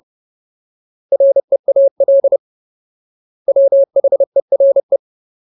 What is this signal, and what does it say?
Morse code, heard as a keyed tone, about 30 wpm, 550 Hz. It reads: REAL WHERE
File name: Ultimate Word List / 30WPM Ultimate